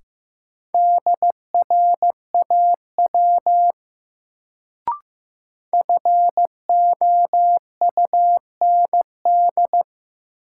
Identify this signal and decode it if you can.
Morse code, heard as a keyed tone, about 15 wpm, 700 Hz. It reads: DRAW FOUND
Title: Sentences from Top 300 Words - New Word Drill - 15wpm